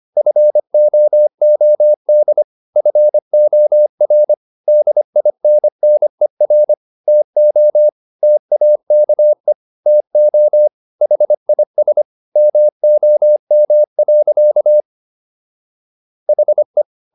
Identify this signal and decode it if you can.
Morse code, heard as a keyed tone, about 25 wpm, 600 Hz. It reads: FOOD FOR DINNER TO TAKE TO HIS MOM. HE